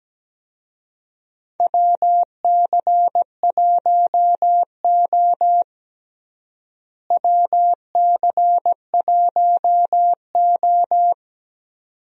Morse code, heard as a keyed tone, about 17 wpm, 700 Hz. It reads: WC1O WC1O
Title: USA Call Signs - 4 Characters - Repeated 2x - 17wpm